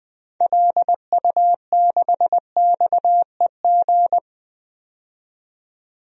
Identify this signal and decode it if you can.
Morse code, heard as a keyed tone, about 20 wpm, 700 Hz. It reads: LU6XEG
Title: DXCC-100 Call Signs - 6 Characters - 20wpm